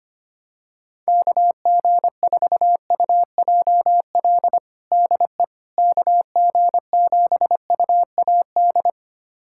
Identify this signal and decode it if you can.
Morse code, heard as a keyed tone, about 25 wpm, 700 Hz. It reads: KG4UJL DE KG7UAD
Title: USA Callsign DE Callsign 25wpm